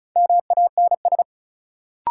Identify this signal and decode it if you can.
Morse code, heard as a keyed tone, about 36 wpm, 700 Hz. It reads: MANS
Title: Sentences from Top 1000 Words - Review Part 6 - New Word Drill - 35wpm